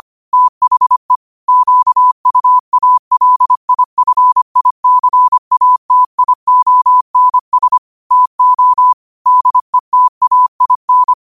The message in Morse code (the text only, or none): THE QUALIFICATIONS TO DETAIN